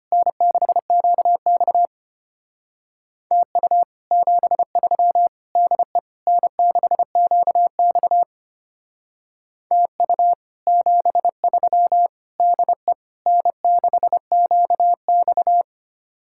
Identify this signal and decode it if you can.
Morse code, heard as a keyed tone, about 29 wpm, 700 Hz. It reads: N6QX TU 73 DE N6QX TU 73 DE N6QX
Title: Speed-Racing Common QSO Phrases Encore - 25wpm